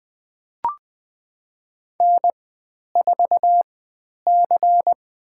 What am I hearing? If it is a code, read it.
Morse code, heard as a keyed tone, about 20 wpm, spaced as for 10 wpm, 700 Hz. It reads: N4C
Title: 095 Lesson 16g - Letter c - Call Signs 20wpm